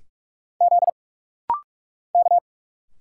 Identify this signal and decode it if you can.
Morse code, heard as a keyed tone, about 46 wpm, 700 Hz. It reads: Z K
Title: Single Letters - Rapid-Fire 45wpm